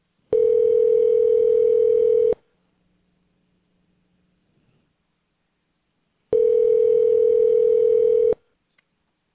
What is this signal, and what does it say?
Ringback tone (North America), 440 + 480 Hz.